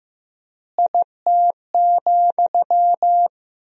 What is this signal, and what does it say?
Morse code, heard as a keyed tone, about 15 wpm, 700 Hz. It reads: IT,
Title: 3x Word-Spacing - Sentences from Top 200 Words - 15wpm